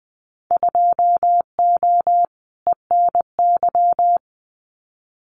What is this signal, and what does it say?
Morse code, heard as a keyed tone, about 19 wpm, 700 Hz. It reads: 2O ENY
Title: Winter Field Day Exchanges Encore 20wpm